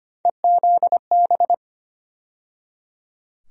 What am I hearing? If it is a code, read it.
Morse code, heard as a keyed tone, about 25 wpm, 700 Hz. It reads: EZB
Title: USA Call Sign Postfixes - 3 Letters 25wpm